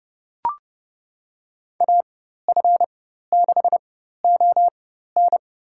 Morse code, heard as a keyed tone, about 30 wpm, spaced as for 14 wpm, 700 Hz. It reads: AF6ON
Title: USA Call Signs - 5 Characters - 4x Character Spacing - 30wpm